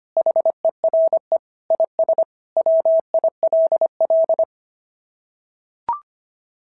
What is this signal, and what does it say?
Morse code, heard as a keyed tone, about 24 wpm, 650 Hz. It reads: HERE IS WILL
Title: SOTA QSO Practice 25wpm